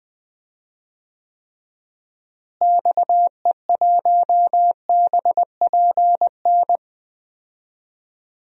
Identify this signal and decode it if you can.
Morse code, heard as a keyed tone, about 20 wpm, 700 Hz. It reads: XE1BPN
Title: DXCC-100 Call Signs - 6 Characters - Repeated 2x - 20wpm